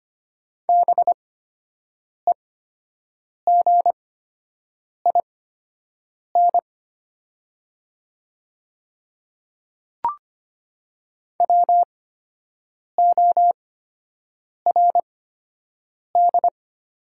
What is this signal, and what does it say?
Morse code, heard as a keyed tone, about 25 wpm, spaced as for 7 wpm, 700 Hz. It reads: BEGIN WORD
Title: 8x Character Spacing - Top 500 Words - 25wpm